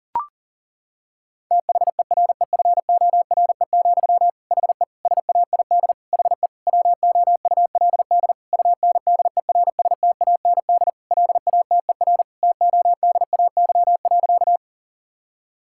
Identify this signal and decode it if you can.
Morse code, heard as a keyed tone, about 40 wpm, 700 Hz. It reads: THEREFORE, HE SAID HE WOULD UNDERSTAND LATER TODAY.
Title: Sentences from Top 1000 Words - Review Part 5 - 40wpm